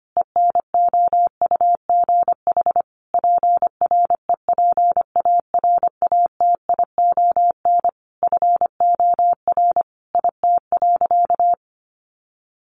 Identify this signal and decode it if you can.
Morse code, heard as a keyed tone, about 25 wpm, 700 Hz. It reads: ENOUGH PREPARATION FOR IT.
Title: Sentences from Top 1400 Words 25wpm